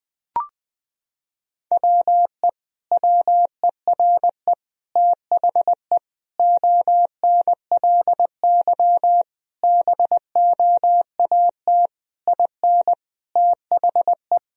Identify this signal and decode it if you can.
Morse code, heard as a keyed tone, about 20 wpm, 700 Hz. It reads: WE WERE THE ONLY BOAT IN THE
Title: Sentences from Top 1000 Words - Review Part 1 - 20wpm